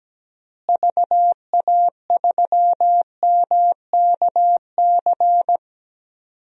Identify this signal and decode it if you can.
Morse code, heard as a keyed tone, about 17 wpm, 700 Hz. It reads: VA3MKC
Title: DXCC-100 Call Signs - 6 Characters - Repeated 3x - 17wpm